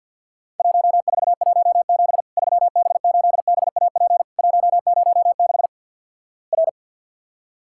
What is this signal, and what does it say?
Morse code, heard as a keyed tone, about 50 wpm, 700 Hz. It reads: 1417 3B8BAP 106 R
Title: Worked All Europe DX Contest - QTC Exchange - 50wpm